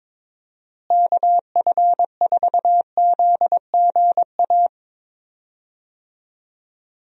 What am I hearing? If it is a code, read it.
Morse code, heard as a keyed tone, about 21 wpm, 700 Hz. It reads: KF4ZGA
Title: USA Call Signs - 6 Characters - Repeated 3x - 22wpm